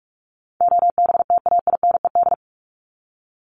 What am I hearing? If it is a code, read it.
Morse code, heard as a keyed tone, about 44 wpm, 700 Hz. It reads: OBTAINED
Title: ED Suffix - Top 100 Words with Base Words 45wpm